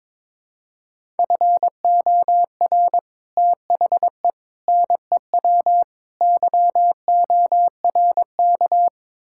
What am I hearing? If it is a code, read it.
Morse code, heard as a keyed tone, about 22 wpm, 700 Hz. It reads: FOR THE NEW YORK
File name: practice / 22wpm / Build A Phrase - 5-Word Phrases - 22wpm